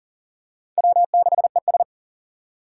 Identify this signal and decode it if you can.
Morse code, heard as a keyed tone, about 40 wpm, 700 Hz. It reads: W6ES
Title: Four Challenging Letter-Number Combinations 40wpm